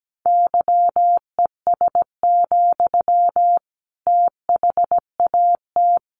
Morse code, heard as a keyed tone, about 17 wpm, 700 Hz. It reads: YES, THAT
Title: Sentences from Top 500 Words 17wpm